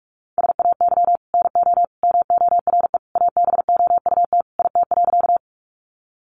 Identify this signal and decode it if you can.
Morse code, heard as a keyed tone, about 44 wpm, 700 Hz. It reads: SAY NO MORE ABOUT IT.
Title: Sentences from Top 200 Words 45wpm